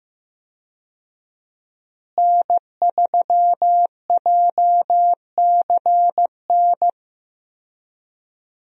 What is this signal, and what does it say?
Morse code, heard as a keyed tone, about 16 wpm, 700 Hz. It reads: N3JCN
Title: USA Call Signs - 5 Characters - Repeated 2x - 15wpm